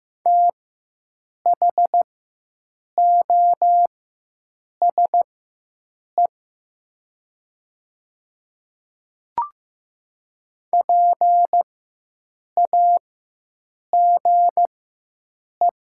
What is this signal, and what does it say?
Morse code, heard as a keyed tone, about 15 wpm, spaced as for 7 wpm, 700 Hz. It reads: THOSE PAGE
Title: 4x Character Spacing - Top 300 Words - 15wpm